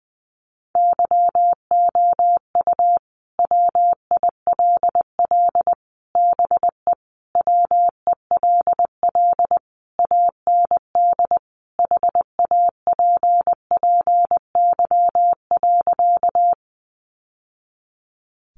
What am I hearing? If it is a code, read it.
Morse code, heard as a keyed tone, about 20 wpm, 700 Hz. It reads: YOU WILL BE WELL AND HAPPY.
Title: Sentences from Top 1000 Words 20wpm